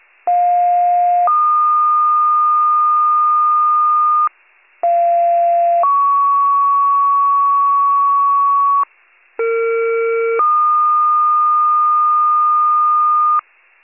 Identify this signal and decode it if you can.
Two-tone page, heard as a one-second tone, then a three-second tone, about 710 then 1220 Hz, about 690 then 1120 Hz, about 460 then 1220 Hz. It is heard from 0.3 to 13.4 s.